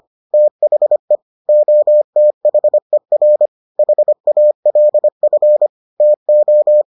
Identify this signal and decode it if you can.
Morse code, heard as a keyed tone, about 25 wpm, 600 Hz. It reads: THE OTHER HALF TO